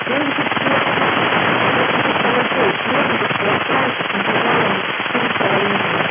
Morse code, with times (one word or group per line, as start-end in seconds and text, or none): none